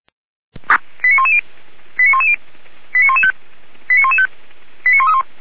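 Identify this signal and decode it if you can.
Five-tone selective call (ZVEI), heard as quick runs of five tones, tones of about 70 ms. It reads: 89110 89110 89115 89115 89131
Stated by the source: ZVEI five-tone selective call received by SDR.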